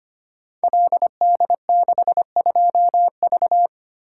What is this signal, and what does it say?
Morse code, heard as a keyed tone, about 25 wpm, 700 Hz. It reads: LD62V
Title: Code Groups - 5 Letter:Numbers - 25wpm